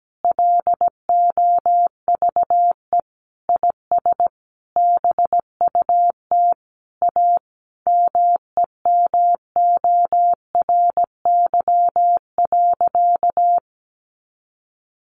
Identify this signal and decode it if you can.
Morse code, heard as a keyed tone, about 17 wpm, 700 Hz. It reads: LOVE IS BUT A MEMORY.
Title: Sentences from Top 1300 Words 17wpm